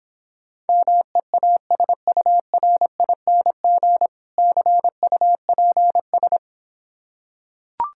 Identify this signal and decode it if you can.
Morse code, heard as a keyed tone, about 26 wpm, 700 Hz. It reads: MEASURING CUPS